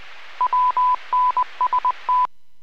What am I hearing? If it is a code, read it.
Morse code, heard as a keyed tone, about 20 wpm, 1000 Hz. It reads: WNST
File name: WPS / 02246 09-24 WPS 451.25 WPS